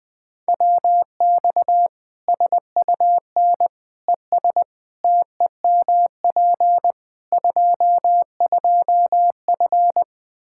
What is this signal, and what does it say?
Morse code, heard as a keyed tone, about 21 wpm, 700 Hz. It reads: WX SUN ES TEMP 22F
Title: Practice QSO - Exchange 2 - Info - 20wpm